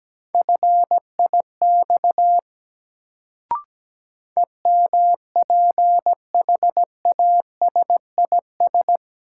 Morse code, heard as a keyed tone, about 17 wpm, 700 Hz. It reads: FIX EMPHASIS
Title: Sentences from Top 1400 Words - New Word Drill - 17wpm